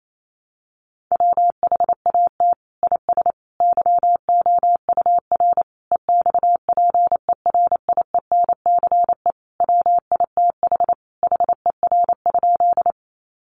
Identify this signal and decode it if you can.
Morse code, heard as a keyed tone, about 28 wpm, 700 Hz. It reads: WHAT IS YOUR EXPERIENCE WITH HER?
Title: Sentences from Top 600 Words 28wpm